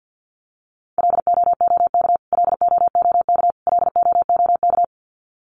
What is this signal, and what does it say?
Morse code, heard as a keyed tone, about 48 wpm, 700 Hz. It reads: LOOK LOOK LOOK
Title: Top 500 Words Repeated 50wpm